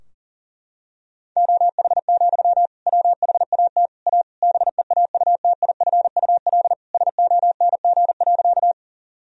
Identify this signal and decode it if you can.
Morse code, heard as a keyed tone, about 40 wpm, 700 Hz. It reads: OH, WHAT A BEAUTIFUL SONG.
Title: Sentences from Top 1000 Words - Review Part 2 - 40wpm